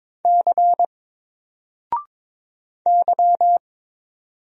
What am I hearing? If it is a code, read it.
Morse code, heard as a keyed tone, about 22 wpm, 700 Hz. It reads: C Y
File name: Single-Letter-Number-22wpm